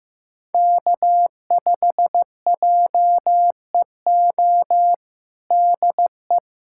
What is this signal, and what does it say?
Morse code, heard as a keyed tone, about 16 wpm, 700 Hz. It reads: K5JEO DE
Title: USA Callsign DE Callsign - 5 Characters - 15wpm